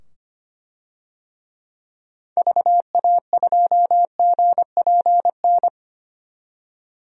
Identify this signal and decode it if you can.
Morse code, heard as a keyed tone, about 25 wpm, 700 Hz. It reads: VA2GPN
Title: DXCC-100 Call Signs - 6 Characters - 25wpm